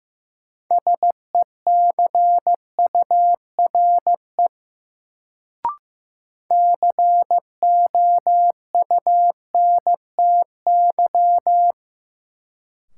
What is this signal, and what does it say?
Morse code, heard as a keyed tone, about 15 wpm, 700 Hz. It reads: SECURE COUNTY
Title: Sentences from Top 1200 Words - New Word Drill Alt - 15wpm